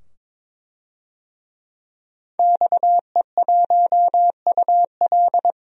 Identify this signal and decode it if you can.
Morse code, heard as a keyed tone, about 22 wpm, 700 Hz. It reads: XE1UL
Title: DXCC-100 Call Signs - 5 Characters - 22wpm